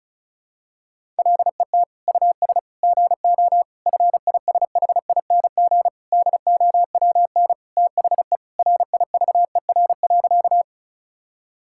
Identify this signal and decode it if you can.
Morse code, heard as a keyed tone, about 36 wpm, 700 Hz. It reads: LET US GO FISHING DOWN THE RIVER.
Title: Sentences from Top 1000 Words - Review Part 5 - 35wpm